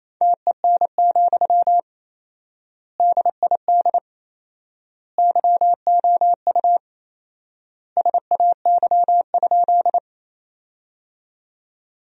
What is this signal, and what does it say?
Morse code, heard as a keyed tone, about 27 wpm, 700 Hz. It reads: TEN, DID YOU SAY?
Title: 4x Word-Spacing - Sentences from Top 400 Words - 28wpm